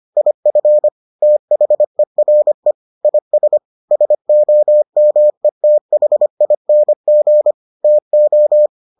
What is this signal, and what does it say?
Morse code, heard as a keyed tone, about 25 wpm, 600 Hz. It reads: IF THERE IS SOMETHING TO